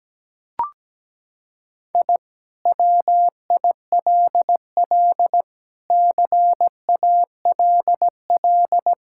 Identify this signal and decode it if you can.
Morse code, heard as a keyed tone, about 17 wpm, 700 Hz. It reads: I WILL CALL
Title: Sentences from Top 1000 Words - Review Part 6 - 17wpm